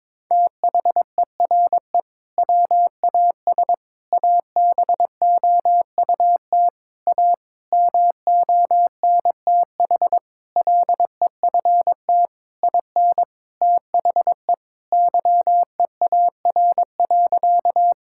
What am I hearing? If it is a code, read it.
Morse code, heard as a keyed tone, about 21 wpm, 700 Hz. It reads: THERE WAS ABOUT A MONTH LEFT IN THE YEAR.
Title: Sentences from Top 1000 Words - Review Part 5 - 22wpm